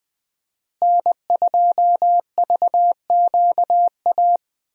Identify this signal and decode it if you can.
Morse code, heard as a keyed tone, about 21 wpm, 700 Hz. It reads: N2VQA
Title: USA Call Signs - 5 Characters - Repeated 2x - 20wpm